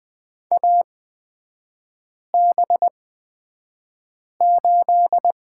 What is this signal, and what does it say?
Morse code, heard as a keyed tone, about 20 wpm, spaced as for 5 wpm, 700 Hz. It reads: AB8
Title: 250 Lesson 40e - BK - Three Characters 20wpm